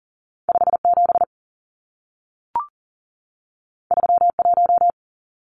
Morse code, heard as a keyed tone, about 40 wpm, 700 Hz. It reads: 57 31
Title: Numbers - 2-Digit 40wpm